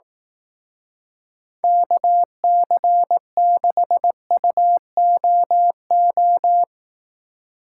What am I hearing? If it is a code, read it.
Morse code, heard as a keyed tone, about 18 wpm, 700 Hz. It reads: KC6UOO
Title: Speed Racing - USA 2x3 Call Signs 15wpm